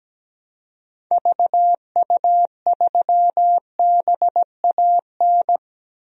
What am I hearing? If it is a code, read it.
Morse code, heard as a keyed tone, about 17 wpm, 700 Hz. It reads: VU3BAN
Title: DXCC-100 Call Signs - 6 Characters - 17wpm